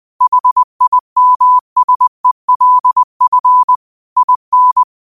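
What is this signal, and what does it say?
Morse code, heard as a keyed tone, about 20 wpm, 1000 Hz. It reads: HIMSELF IN
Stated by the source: Morse quote of the day sent at 20 wpm.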